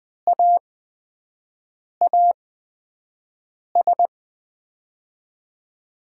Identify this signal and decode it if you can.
Morse code, heard as a keyed tone, about 20 wpm, spaced as for 5 wpm, 700 Hz. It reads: AAS
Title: USA Call Signs - 6 Characters - 8x Character Spacing - 20wpm